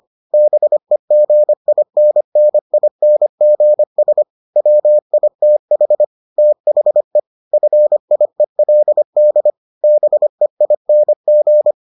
Morse code, heard as a keyed tone, about 25 wpm, 600 Hz. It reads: BEGINNINGS WITH THE FIELD BEING